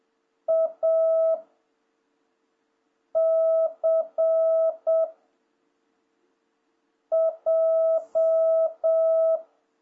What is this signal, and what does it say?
Morse code, heard as a keyed tone, about 7 wpm, 660 Hz. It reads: A C J
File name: jh8cbh / 2026cw